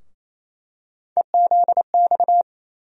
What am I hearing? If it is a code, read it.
Morse code, heard as a keyed tone, about 28 wpm, 700 Hz. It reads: EZX
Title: USA Call Sign Postfixes - 3 Letters 28wpm